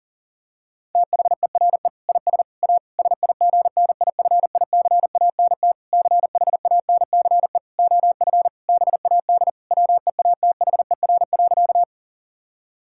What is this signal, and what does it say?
Morse code, heard as a keyed tone, about 37 wpm, 700 Hz. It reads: THERE IS A SIGNIFICANT CHANCE OF BAD WEATHER.
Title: Sentences from Top 1000 Words - Review Part 6 - 40wpm